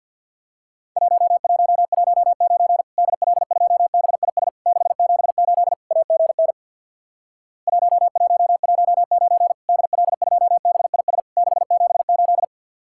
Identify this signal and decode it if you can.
Morse code, heard as a keyed tone, about 50 wpm, 700 Hz. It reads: 1119 DL2BIS 678 AGN 1119 DL2BIS 678
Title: Worked All Europe DX Contest - QTC Exchange with Repeats - 50wpm